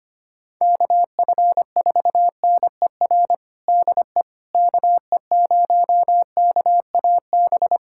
Morse code, heard as a keyed tone, about 25 wpm, 700 Hz. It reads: KF4NER DE KE0KAB
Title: USA Callsign DE Callsign 25wpm